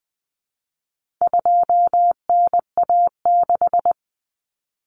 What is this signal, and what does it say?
Morse code, heard as a keyed tone, about 20 wpm, 700 Hz. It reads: 2NA6
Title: Common Cut Numbers - 4-Digit 20wpm